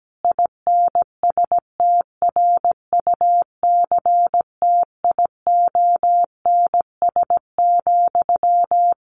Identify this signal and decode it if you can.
Morse code, heard as a keyed tone, about 17 wpm, 700 Hz. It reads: INSTRUCTIONS,
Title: A Sound Of Thunder 17wpm